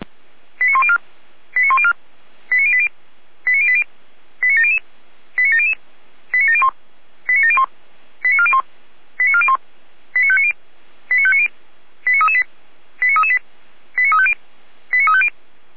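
Five-tone selective call (ZVEI), heard as quick runs of five tones, tones of about 70 ms. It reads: 89104 89104 89080 89080 89700 89700 89701 89701 89401 89401 89590 89590 89338 89338 89350 89350
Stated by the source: ZVEI five-tone selective call 89262 received by SDR and logged by Feuerwehr Orlamünde, a fire brigade.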